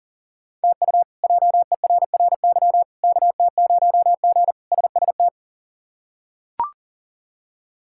Morse code, heard as a keyed tone, about 37 wpm, 700 Hz. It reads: TU JERRY KT0G SST
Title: K1USN SST QSO Practice - 40wpm